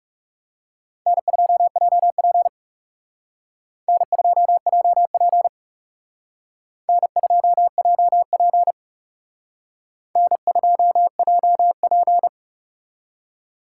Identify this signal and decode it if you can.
Morse code, heard as a keyed tone, about 34 wpm, 700 Hz. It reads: N2JP N2JP N2JP N2JP
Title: Speed Racing - USA 1x2 Call Signs 30wpm